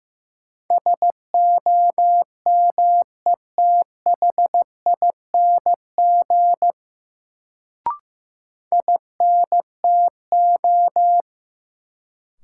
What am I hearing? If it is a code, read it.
Morse code, heard as a keyed tone, about 15 wpm, 700 Hz. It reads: SOMETHING INTO
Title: Sentences from Top 600 Words - New Word Drill - 15wpm